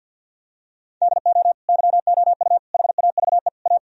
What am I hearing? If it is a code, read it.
Morse code, heard as a keyed tone, about 46 wpm, 700 Hz. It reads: DO YOU HAVE A
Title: Sentences from Top 1000 Words - Review Part 4 - 50wpm